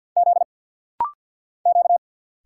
Morse code, heard as a keyed tone, about 50 wpm, 700 Hz. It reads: Z Q
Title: Single Letters - Rapid-Fire 50wpm